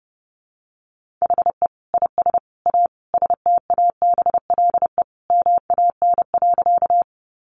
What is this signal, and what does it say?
Morse code, heard as a keyed tone, about 30 wpm, 700 Hz. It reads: HE IS A STABLE MAN.
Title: Sentences from Top 1500 Words 30wpm